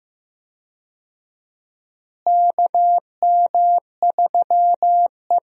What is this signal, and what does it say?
Morse code, heard as a keyed tone, about 15 wpm, 700 Hz. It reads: KM3E
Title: USA Call Signs - 4 Characters - Repeated 3x - 15wpm